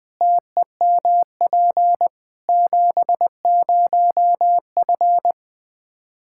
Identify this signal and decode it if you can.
Morse code, heard as a keyed tone, about 20 wpm, 700 Hz. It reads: TEMP 70F
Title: Practice QSO - Exchange 2 - Info - 20wpm